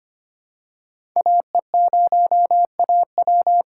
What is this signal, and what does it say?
Morse code, heard as a keyed tone, about 25 wpm, 700 Hz. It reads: AE0AW
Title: USA Call Signs - 5 Characters - 6x Character Spacing - 25wpm